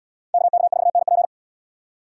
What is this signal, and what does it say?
Morse code, heard as a keyed tone, about 77 wpm, 700 Hz. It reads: KK4NP